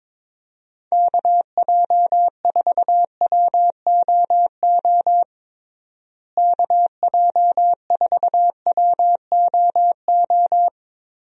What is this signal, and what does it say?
Morse code, heard as a keyed tone, about 22 wpm, 700 Hz. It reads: KJ4WOO KJ4WOO
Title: USA Call Signs - 6 Characters - Repeated 3x - 22wpm